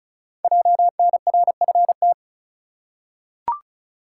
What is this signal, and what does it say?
Morse code, heard as a keyed tone, about 36 wpm, 700 Hz. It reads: JNRFT
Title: Code Groups - 5 Letter:Numbers - 35wpm